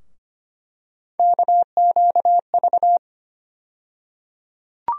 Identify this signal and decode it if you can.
Morse code, heard as a keyed tone, about 25 wpm, 700 Hz. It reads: KQV
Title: USA Call Sign Postfixes - 3 Letters 25wpm